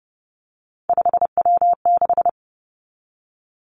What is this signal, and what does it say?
Morse code, heard as a keyed tone, about 30 wpm, 700 Hz. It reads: 5W6
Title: Three Challenging Letter-Number Combinations 30wpm